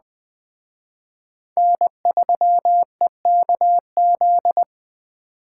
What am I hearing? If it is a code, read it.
Morse code, heard as a keyed tone, about 20 wpm, 700 Hz. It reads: N3EKZ
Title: Speed Racing - USA 1x3 Call Signs 17wpm